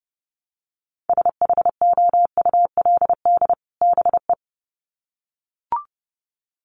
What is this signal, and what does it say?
Morse code, heard as a keyed tone, about 28 wpm, 700 Hz. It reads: SHOULD BE
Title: Sets of 2 Words 30wpm